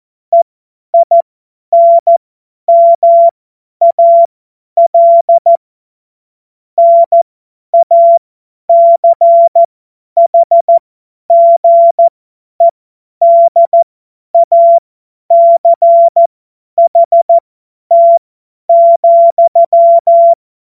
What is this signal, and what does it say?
Morse code, heard as a keyed tone, about 14 wpm, spaced as for 10 wpm, 680 Hz. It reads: EINMAL NACHGEDACHT,
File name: Morsekurs / Witze / Witz 13. Januar 2026 50 F70